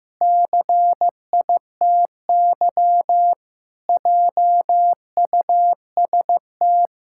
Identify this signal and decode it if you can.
Morse code, heard as a keyed tone, about 15 wpm, 700 Hz. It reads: CITY JUST
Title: Pangram Challenge v2 - 15wpm